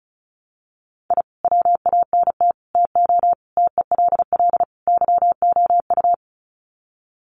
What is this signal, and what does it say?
Morse code, heard as a keyed tone, about 34 wpm, 700 Hz. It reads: I WANT TO TELL YOU
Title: Sets of 5 Words Encore 35wpm